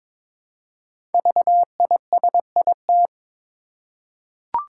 Morse code, heard as a keyed tone, about 22 wpm, 700 Hz. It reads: VISIT
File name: 5 Letter Words - First 3 Letters Given - 22wpm